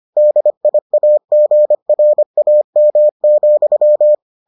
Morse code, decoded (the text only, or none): DIAGRAM,